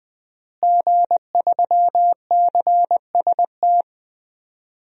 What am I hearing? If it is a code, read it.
Morse code, heard as a keyed tone, about 20 wpm, 700 Hz. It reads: G3CST
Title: DXCC-100 Call Signs - 5 Characters - Repeated 3x - 20wpm